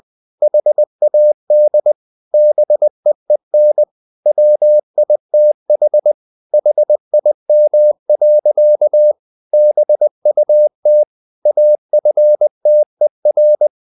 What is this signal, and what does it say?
Morse code, heard as a keyed tone, about 20 wpm, 600 Hz. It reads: HAD BEEN WITH HIM. BUT AFTER
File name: the old man and the sea / the old man and the sea part 1 20wpm